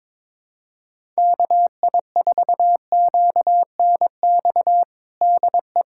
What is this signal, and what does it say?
Morse code, heard as a keyed tone, about 22 wpm, 700 Hz. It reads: KI4QNX DE
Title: USA Callsign DE Callsign - 6 Characters - 22wpm